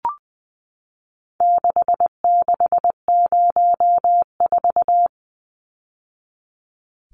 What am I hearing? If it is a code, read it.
Morse code, heard as a keyed tone, about 19 wpm, 700 Hz. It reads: 6604
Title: Numbers - 4-Digit 20wpm